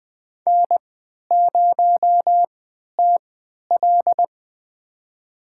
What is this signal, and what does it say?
Morse code, heard as a keyed tone, about 20 wpm, spaced as for 11 wpm, 700 Hz. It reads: N0TL
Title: USA Call Signs - 4 Characters - 3x Character Spacing - 20wpm